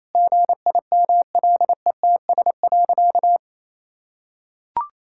Morse code, heard as a keyed tone, about 28 wpm, 700 Hz. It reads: GIMLETS.
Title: Pangram Challenge v2 - 28wpm